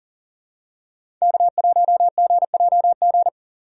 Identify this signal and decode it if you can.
Morse code, heard as a keyed tone, about 38 wpm, 700 Hz. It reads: K1GJG
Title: USA Call Signs - 5 Characters - 3x Character Spacing - 40wpm